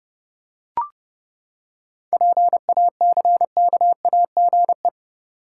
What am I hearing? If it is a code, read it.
Morse code, heard as a keyed tone, about 30 wpm, 700 Hz. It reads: PACKAGE
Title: Seven Letter Words - 30wpm